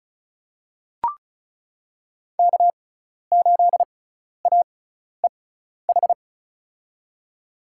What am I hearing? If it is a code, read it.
Morse code, heard as a keyed tone, about 36 wpm, spaced as for 12 wpm, 700 Hz. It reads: K8AEH
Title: USA Call Signs - 5 Characters - 6x Character Spacing - 35wpm